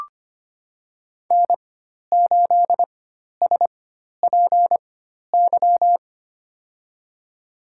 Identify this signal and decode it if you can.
Morse code, heard as a keyed tone, about 25 wpm, spaced as for 12 wpm, 700 Hz. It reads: N8SPY
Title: USA Call Signs - 5 Characters - 4x Character Spacing - 25wpm